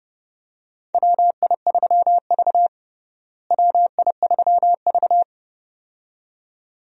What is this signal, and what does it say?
Morse code, heard as a keyed tone, about 32 wpm, 700 Hz. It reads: WI3V WI3V
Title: USA Call Signs - 4 Characters - Repeated 2x - 30wpm